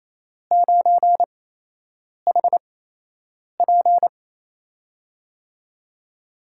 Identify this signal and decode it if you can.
Morse code, heard as a keyed tone, about 28 wpm, spaced as for 8 wpm, 700 Hz. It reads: 9HP
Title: USA Call Signs - 8x Character Spacing - 28wpm